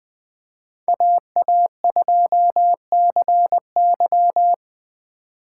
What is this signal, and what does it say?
Morse code, heard as a keyed tone, about 20 wpm, 700 Hz. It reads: AA2CY
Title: Speed Racing - USA 2x2 Call Signs 17wpm